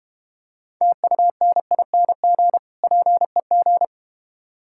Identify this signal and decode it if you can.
Morse code, heard as a keyed tone, about 32 wpm, 700 Hz. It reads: TUNING PEG